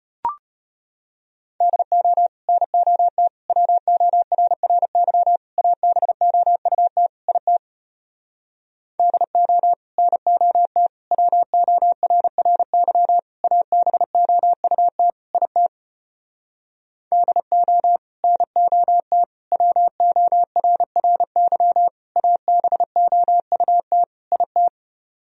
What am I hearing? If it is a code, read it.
Morse code, heard as a keyed tone, about 33 wpm, 700 Hz. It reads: DO NOT WORRY ABOUT IT DO NOT WORRY ABOUT IT DO NOT WORRY ABOUT IT